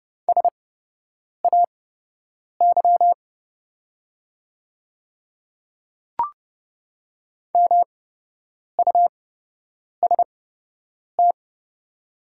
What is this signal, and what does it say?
Morse code, heard as a keyed tone, about 30 wpm, spaced as for 8 wpm, 700 Hz. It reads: SAY MUST
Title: 8x Character Spacing - Top 300 Words - 30wpm